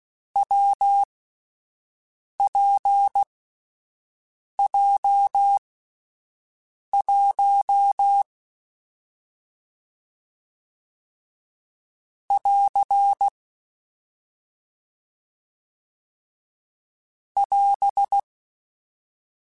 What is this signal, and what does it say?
Morse code, heard as a keyed tone, about 16 wpm, spaced as for 5 wpm, 800 Hz. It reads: WPJ1 <AR> <AS>
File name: Difficult Combos / Difficult Combos-mp3s / WPJ WPJ1 @5